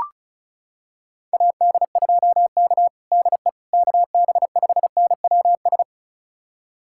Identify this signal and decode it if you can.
Morse code, heard as a keyed tone, about 34 wpm, 700 Hz. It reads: AD2K DE KB5NWS
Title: USA Callsign DE Callsign 35wpm